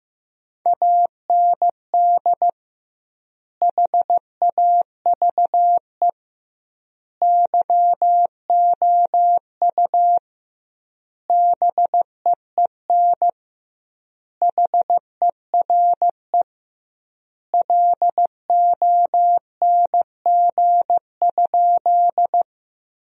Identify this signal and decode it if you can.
Morse code, heard as a keyed tone, about 16 wpm, 700 Hz. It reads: AND HAVE YOU BEEN HERE LONG?
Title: 2x Word-Spacing - Sentences from Top 200 Words - 15wpm